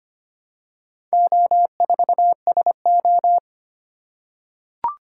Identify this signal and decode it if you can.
Morse code, heard as a keyed tone, about 25 wpm, 700 Hz. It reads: O4SO
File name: practice / 25wpm / Four Letter-Number Combinations - 25wpm